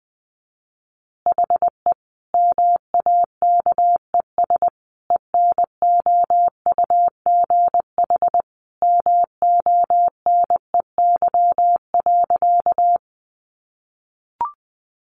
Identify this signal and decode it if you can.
Morse code, heard as a keyed tone, about 20 wpm, 700 Hz. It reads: HE MAKES ENOUGH MONEY.
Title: Additional Easy Phrases 20wpm